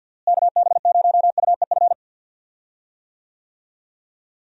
Morse code, heard as a keyed tone, about 50 wpm, 700 Hz. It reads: KB0UEF
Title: USA Call Signs - 6 Characters - Repeated 2x - 50wpm